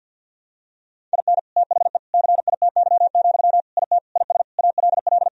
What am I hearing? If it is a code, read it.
Morse code, heard as a keyed tone, about 51 wpm, 700 Hz. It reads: IN THE CITY, IT IS ALL
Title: Sentences from Top 1000 Words - Review Part 3 - 50wpm